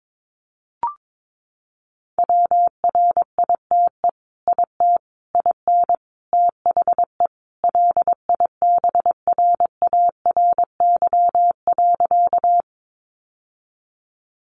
Morse code, heard as a keyed tone, about 22 wpm, 700 Hz. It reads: WRITE IT IN THE LIBRARY.
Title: Sentences from Top 600 Words 22wpm